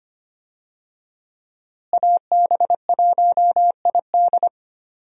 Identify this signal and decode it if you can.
Morse code, heard as a keyed tone, about 25 wpm, 700 Hz. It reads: AB1ID
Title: USA Call Signs - 5 Characters - Repeated 3x - 25wpm